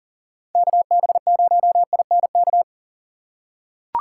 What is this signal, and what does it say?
Morse code, heard as a keyed tone, about 39 wpm, 700 Hz. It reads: KB0INK
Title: USA Call Signs - 3x Character Spacing - 40wpm